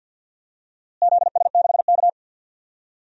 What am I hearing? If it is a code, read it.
Morse code, heard as a keyed tone, about 45 wpm, 700 Hz. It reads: ZS6K
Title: DXCC-100 Call Signs - 4 Characters - Repeated 2x - 50wpm